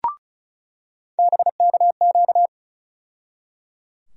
Morse code, heard as a keyed tone, about 34 wpm, 700 Hz. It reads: BKQ
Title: USA Call Sign Postfixes - 3 Letters 35wpm